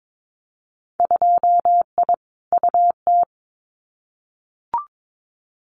Morse code, heard as a keyed tone, about 22 wpm, 700 Hz. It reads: 2I UT
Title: Winter Field Day Exchanges 22wpm